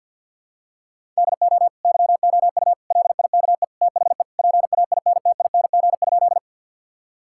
Morse code, heard as a keyed tone, about 52 wpm, 700 Hz. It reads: DO YOU LIKE THE PAINTING?
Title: Sentences from Top 1000 Words - Review Part 6 - 50wpm